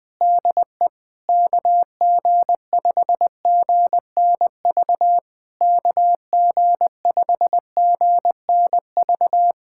Morse code, heard as a keyed tone, about 20 wpm, 690 Hz. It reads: DE KG5GNV KG5GNV
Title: SOTA QSO Practice 20wpm